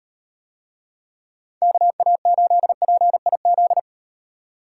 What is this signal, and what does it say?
Morse code, heard as a keyed tone, about 36 wpm, 700 Hz. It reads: KA8PIZ